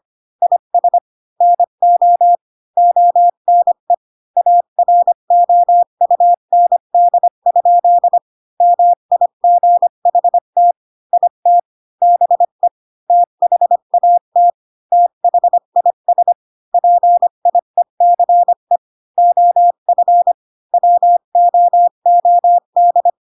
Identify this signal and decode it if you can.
Morse code, heard as a keyed tone, about 25 wpm, 700 Hz. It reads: IS NO ONE AROUND? MIGHT IT BE THAT THIS PIECE OF WOOD
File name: Pinocchio / Pinocchio ch1 20 25 ik1hge